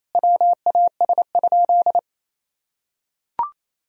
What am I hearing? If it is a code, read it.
Morse code, heard as a keyed tone, about 28 wpm, 700 Hz. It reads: WAS?
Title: Sentences from Top 200 Words and Abbreviations 28wpm